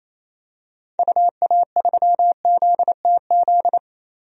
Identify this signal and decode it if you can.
Morse code, heard as a keyed tone, about 28 wpm, 700 Hz. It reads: UA3ZTZ
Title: DXCC-100 Call Signs - 6 Characters - Repeated 2x - 28wpm